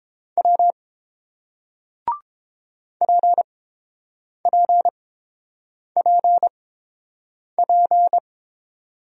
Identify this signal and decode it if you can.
Morse code, heard as a keyed tone, about 30 wpm, spaced as for 7 wpm, 700 Hz. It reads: W PPPP